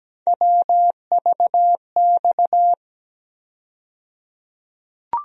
Morse code, heard as a keyed tone, about 17 wpm, 700 Hz. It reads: WVX
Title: USA Call Sign Postfixes - 3 Letters 17wpm